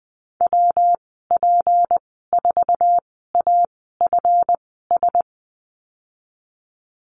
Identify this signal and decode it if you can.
Morse code, heard as a keyed tone, about 20 wpm, spaced as for 15 wpm, 700 Hz. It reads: WP4AFS
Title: 161 Lesson 26g - Letter g - Call Signs 20wpm